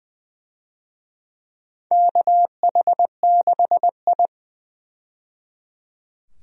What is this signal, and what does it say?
Morse code, heard as a keyed tone, about 20 wpm, 700 Hz. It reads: KH6I
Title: DXCC-100 Call Signs - 4 Characters - 20wpm